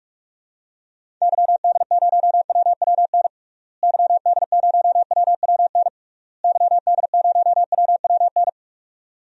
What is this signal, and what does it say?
Morse code, heard as a keyed tone, about 42 wpm, 700 Hz. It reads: YD0WWN YD0WWN YD0WWN
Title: DXCC-100 Call Signs - 6 Characters - Repeated 3x - 45wpm